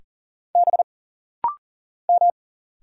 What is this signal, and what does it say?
Morse code, heard as a keyed tone, about 40 wpm, 700 Hz. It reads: B M
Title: Single Letters - Rapid-Fire 40wpm